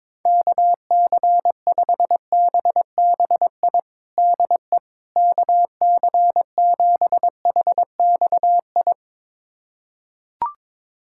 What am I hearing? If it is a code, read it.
Morse code, heard as a keyed tone, about 22 wpm, 700 Hz. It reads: KC5BBI DE KC7HXI
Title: USA Callsign DE Callsign - 6 Characters - 22wpm